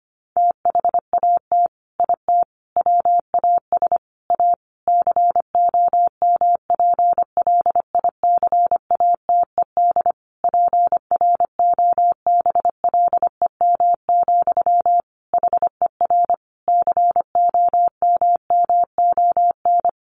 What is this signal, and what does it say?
Morse code, heard as a keyed tone, about 25 wpm, 700 Hz. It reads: THAT IT WAS A COMPLICATED PROBLEM, HER COMMON
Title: Sentences from Top 1200 Words 25wpm